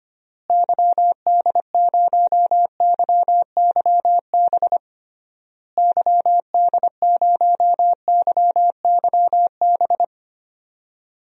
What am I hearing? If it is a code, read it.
Morse code, heard as a keyed tone, about 25 wpm, 700 Hz. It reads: YD0YYB YD0YYB
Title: DXCC-100 Call Signs - 6 Characters - Repeated 3x - 25wpm